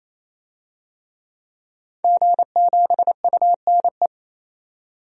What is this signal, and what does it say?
Morse code, heard as a keyed tone, about 28 wpm, 700 Hz. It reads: G7UNE
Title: DXCC-100 Call Signs - 5 Characters - 28wpm